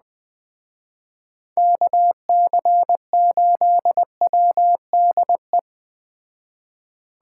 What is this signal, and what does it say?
Morse code, heard as a keyed tone, about 21 wpm, 700 Hz. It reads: KC8WDE
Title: Speed Racing - USA 2x3 Call Signs 17wpm